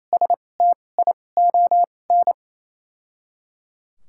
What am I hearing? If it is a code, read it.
Morse code, heard as a keyed tone, about 28 wpm, spaced as for 20 wpm, 700 Hz. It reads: STION
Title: 2x Character Spacing - Top 500 Words - 28wpm